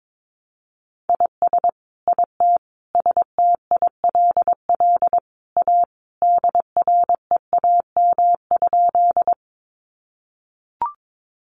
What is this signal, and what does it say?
Morse code, heard as a keyed tone, about 22 wpm, 700 Hz. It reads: IS IT STILL A DREAM?
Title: Sentences from Top 1500 Words 22wpm